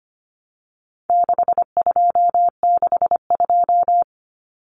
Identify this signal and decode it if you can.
Morse code, heard as a keyed tone, about 25 wpm, 700 Hz. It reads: 6262
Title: Numbers - 4-Digit 25wpm